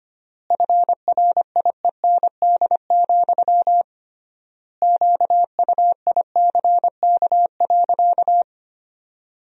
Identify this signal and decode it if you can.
Morse code, heard as a keyed tone, about 25 wpm, 700 Hz. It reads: FRIEND, QUICK.
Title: 3x Word-Spacing - Sentences from Top 500 Words - 25wpm